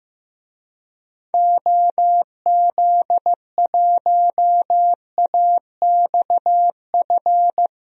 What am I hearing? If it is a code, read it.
Morse code, heard as a keyed tone, about 15 wpm, 700 Hz. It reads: OZ1AXF
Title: DXCC-100 Call Signs - 6 Characters - 15wpm